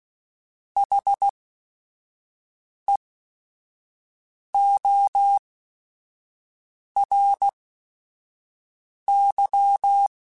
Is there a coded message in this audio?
Yes: Morse code from 0.8 to 10.1 s, heard as a keyed tone, about 16 wpm, spaced as for 5 wpm, 800 Hz. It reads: HEORY